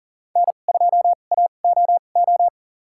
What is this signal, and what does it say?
Morse code, heard as a keyed tone, about 40 wpm, spaced as for 29 wpm, 700 Hz. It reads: N2AOO
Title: USA Call Signs - 5 Characters - 2x Character Spacing - 40wpm